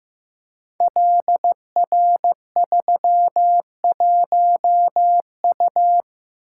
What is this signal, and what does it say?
Morse code, heard as a keyed tone, about 15 wpm, 700 Hz. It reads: LR31U
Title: Code Groups - 5 Letter:Numbers - 15wpm